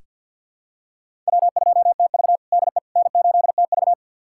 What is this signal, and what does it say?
Morse code, heard as a keyed tone, about 46 wpm, 690 Hz. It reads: W2TV DE N8TV
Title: USA Callsign DE Callsign - 4 Characters - 50wpm